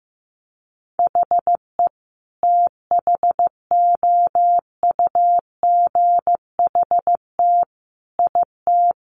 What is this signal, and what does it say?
Morse code, heard as a keyed tone, about 15 wpm, 700 Hz. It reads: HE THOUGHT IT
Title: Sentences from Top 1200 Words 15wpm